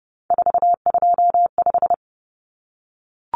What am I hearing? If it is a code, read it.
Morse code, heard as a keyed tone, about 30 wpm, 700 Hz. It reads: 425
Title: Numbers - 3-Digit 30wpm